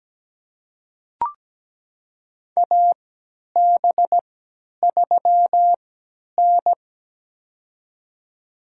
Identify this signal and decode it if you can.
Morse code, heard as a keyed tone, about 17 wpm, spaced as for 10 wpm, 700 Hz. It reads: AB3N
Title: USA Call Signs - 4 Characters - 3x Character Spacing - 17wpm